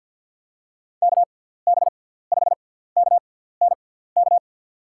Morse code, heard as a keyed tone, about 50 wpm, spaced as for 17 wpm, 700 Hz. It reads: KB5KNK
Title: USA Call Signs - 6 Characters - 6x Character Spacing - 50wpm